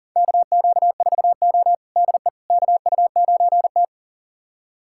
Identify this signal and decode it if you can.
Morse code, heard as a keyed tone, about 40 wpm, 700 Hz. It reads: KQ4O DE KU9T
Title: USA Callsign DE Callsign - 4 Characters - 40wpm